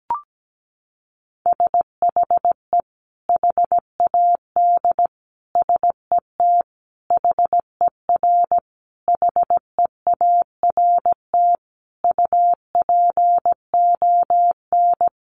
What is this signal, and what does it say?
Morse code, heard as a keyed tone, about 17 wpm, 700 Hz. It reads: SHE HAD SET HER HEART UPON
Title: Sentences from Top 1200 Words 17wpm